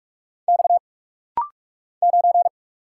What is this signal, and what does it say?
Morse code, heard as a keyed tone, about 46 wpm, 700 Hz. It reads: X 9
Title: Single Letter-Number - Rapid-Fire 45wpm